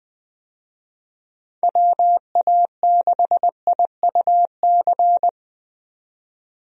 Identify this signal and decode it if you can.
Morse code, heard as a keyed tone, about 21 wpm, 700 Hz. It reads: WA6IUC
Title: USA Call Signs - 6 Characters - 20wpm